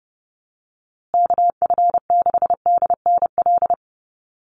Morse code, heard as a keyed tone, about 30 wpm, 700 Hz. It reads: KF6DNL
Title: USA Call Signs Repeated 3x 30wpm